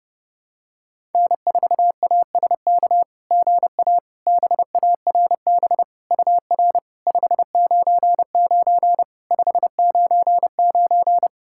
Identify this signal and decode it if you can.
Morse code, heard as a keyed tone, about 32 wpm, 690 Hz. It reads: N4ASK GA BARB UR 599 599
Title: SOTA QSO Practice 30wpm